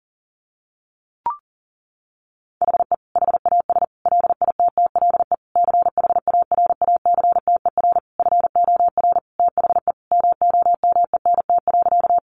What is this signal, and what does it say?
Morse code, heard as a keyed tone, about 40 wpm, 700 Hz. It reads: HE HAS LITTLE CHARACTER FOR THE MOMENT.
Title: Sentences from Top 1200 Words 40wpm